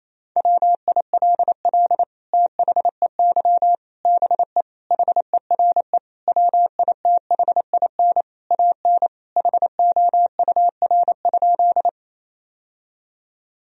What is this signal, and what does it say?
Morse code, heard as a keyed tone, about 28 wpm, 700 Hz. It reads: WILL THEY BE HERE WITHIN AN HOUR?
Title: Sentences from Top 1000 Words - Review Part 4 - 28wpm